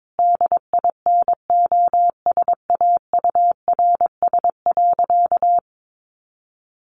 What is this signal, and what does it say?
Morse code, heard as a keyed tone, about 22 wpm, 700 Hz. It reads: DINOSAURS.
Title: A Sound Of Thunder 22wpm